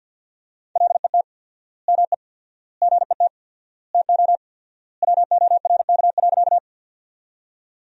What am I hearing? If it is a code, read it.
Morse code, heard as a keyed tone, about 50 wpm, 700 Hz. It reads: LET ME GET TO WORK.
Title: 4x Word-Spacing - Sentences from Top 300 Words - 50wpm